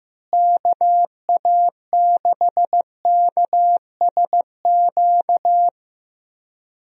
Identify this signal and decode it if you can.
Morse code, heard as a keyed tone, about 15 wpm, 700 Hz. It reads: KA6KSQ
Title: USA Call Signs - 6 Characters - Repeated 3x - 15wpm